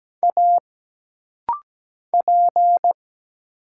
Morse code, heard as a keyed tone, about 17 wpm, 700 Hz. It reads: A P
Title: Single Letter-Number Alt - Rapid-Fire 17wpm